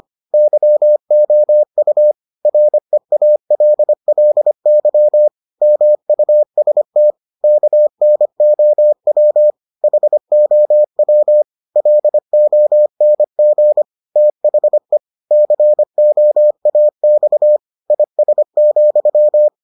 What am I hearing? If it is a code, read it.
Morse code, heard as a keyed tone, about 25 wpm, 600 Hz. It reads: YOU REALLY MUST KNOW HOW LONG THE COAX IS,